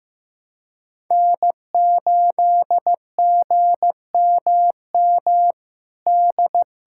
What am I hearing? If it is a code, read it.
Morse code, heard as a keyed tone, about 15 wpm, 700 Hz. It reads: N8GMM D
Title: USA Callsign DE Callsign - 5 Characters - 15wpm